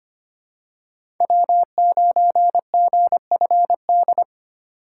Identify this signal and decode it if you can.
Morse code, heard as a keyed tone, about 25 wpm, 700 Hz. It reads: W9GFD
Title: USA Call Signs - 5 Characters - 8x Character Spacing - 25wpm